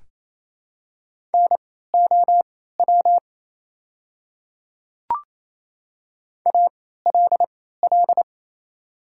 Morse code, heard as a keyed tone, about 28 wpm, spaced as for 16 wpm, 690 Hz. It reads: NOW ALL
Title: 3x Character Spacing - Top 500 Words - 28wpm